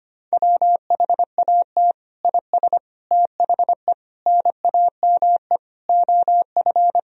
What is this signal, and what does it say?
Morse code, heard as a keyed tone, about 25 wpm, 700 Hz. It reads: WHAT IS THE NAME OF
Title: Sentences from Top 1000 Words - Review Part 2 - 25wpm